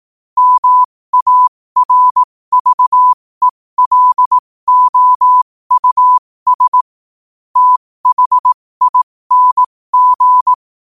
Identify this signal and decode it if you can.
Morse code, heard as a keyed tone, about 18 wpm, spaced as for 15 wpm, 1000 Hz. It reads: MARVELOUS THING
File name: 2024-11-18-15wpm